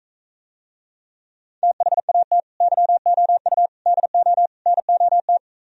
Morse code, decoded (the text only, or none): THAT YOU DO NOT